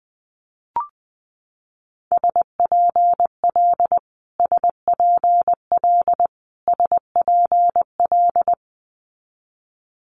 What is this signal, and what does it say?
Morse code, heard as a keyed tone, about 20 wpm, 700 Hz. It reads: SPL SPL SPL
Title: Multi-letter Phonemes Repeated 3x 20wpm